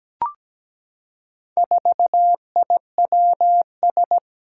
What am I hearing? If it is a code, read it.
Morse code, heard as a keyed tone, about 17 wpm, 700 Hz. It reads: 4IWS
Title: Four Challenging Letter-Number Combinations 17wpm